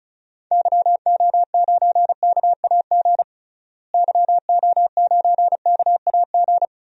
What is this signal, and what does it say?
Morse code, heard as a keyed tone, about 36 wpm, 700 Hz. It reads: YO9KAG YO9KAG
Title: DXCC-100 Call Signs - 6 Characters - Repeated 3x - 35wpm